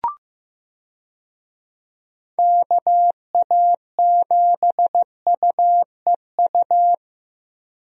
Morse code, heard as a keyed tone, about 16 wpm, 700 Hz. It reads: KA7UEU
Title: USA Call Signs - 6 Characters - Repeated 2x - 15wpm